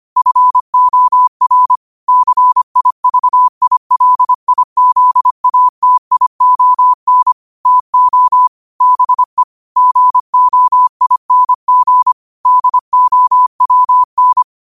Morse code, heard as a keyed tone, about 25 wpm, 1000 Hz. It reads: FOR CIVILIZATION TO BE GOING DOWN